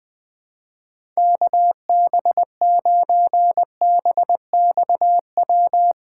Morse code, heard as a keyed tone, about 20 wpm, 700 Hz. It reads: KB9BXW
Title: USA Call Signs - 6 Characters - 2x Character Spacing - 20wpm